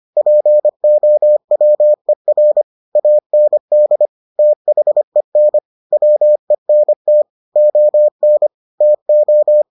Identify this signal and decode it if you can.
Morse code, heard as a keyed tone, about 25 wpm, 600 Hz. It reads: POWER AND THEN WENT ON TO